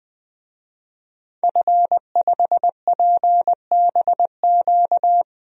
Morse code, heard as a keyed tone, about 21 wpm, 700 Hz. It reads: F5PBQ
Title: DXCC-100 Call Signs - 5 Characters - 20wpm